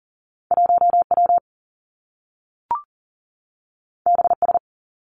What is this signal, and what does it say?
Morse code, heard as a keyed tone, about 42 wpm, 700 Hz. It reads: 1W BS
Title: Two Challenging Letter-Number Combinations 40wpm